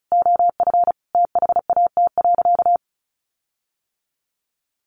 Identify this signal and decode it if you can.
Morse code, heard as a keyed tone, about 34 wpm, 700 Hz. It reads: OF THAT.
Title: Sentences from Top 1500 Words 35wpm